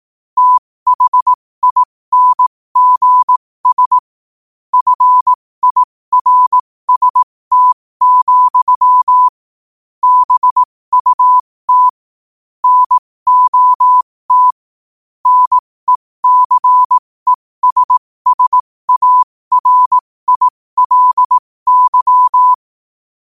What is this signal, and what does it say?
Morse code, heard as a keyed tone, about 18 wpm, spaced as for 15 wpm, 1000 Hz. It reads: THINGS FIRST, BUT NOT NECESSARILY
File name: qotdfiles / 15wpm